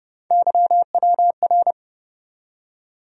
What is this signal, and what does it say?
Morse code, heard as a keyed tone, about 32 wpm, 700 Hz. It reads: YWR
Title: USA Call Sign Postfixes - 3 Letters 30wpm